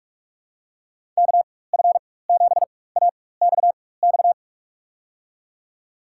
Morse code, heard as a keyed tone, about 41 wpm, spaced as for 20 wpm, 700 Hz. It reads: KF7AXX
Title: USA Call Signs - 6 Characters - 4x Character Spacing - 45wpm